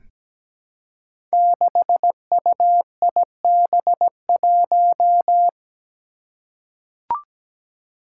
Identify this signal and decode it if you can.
Morse code, heard as a keyed tone, about 17 wpm, 700 Hz. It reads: 6UIB1
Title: Five Challenging Letter-Number Combinations 17wpm